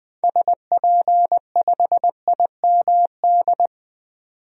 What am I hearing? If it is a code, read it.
Morse code, heard as a keyed tone, about 21 wpm, 700 Hz. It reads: SP5IMD
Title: DXCC-100 Call Signs - 6 Characters - 20wpm